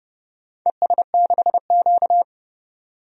Morse code, heard as a keyed tone, about 30 wpm, 700 Hz. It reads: ES6Q
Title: DXCC-100 Call Signs - 4 Characters - 30wpm